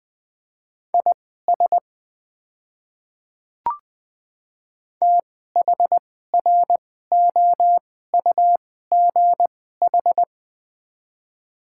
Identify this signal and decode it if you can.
Morse code, heard as a keyed tone, about 21 wpm, spaced as for 15 wpm, 700 Hz. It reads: IS THROUGH
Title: 2x Character Spacing - Top 200 Words - 20wpm